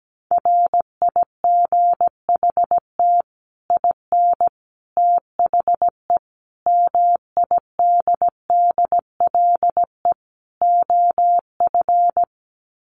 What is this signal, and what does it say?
Morse code, heard as a keyed tone, about 17 wpm, 700 Hz. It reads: RIGHT IN THE MIDDLE OF
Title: Sets of 5 Words 17wpm